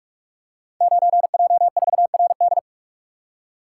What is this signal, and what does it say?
Morse code, heard as a keyed tone, about 44 wpm, 700 Hz. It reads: 9J4RD
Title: Code Groups - 5 Letter:Numbers - 45wpm